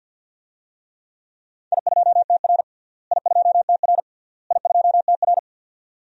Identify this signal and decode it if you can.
Morse code, heard as a keyed tone, about 51 wpm, 700 Hz. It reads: I2TR I2TR I2TR
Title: DXCC-100 Call Signs - 4 Characters - Repeated 3x - 50wpm